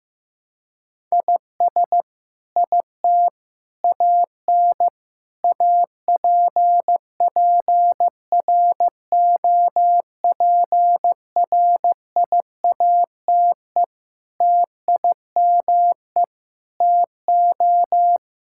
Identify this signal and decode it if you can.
Morse code, heard as a keyed tone, about 16 wpm, 700 Hz. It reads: IS IT AN APPROPRIATE TIME TO
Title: Sentences from Top 1000 Words - Review Part 5 - 15wpm